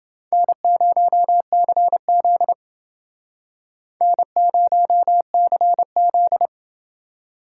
Morse code, heard as a keyed tone, about 29 wpm, 700 Hz. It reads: N0CZ N0CZ